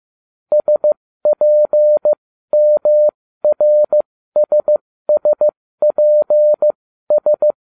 Morse code, heard as a keyed tone, about 15 wpm, spaced as for 13 wpm, 600 Hz. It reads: SPMRSSPS
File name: cw / ubacw07 1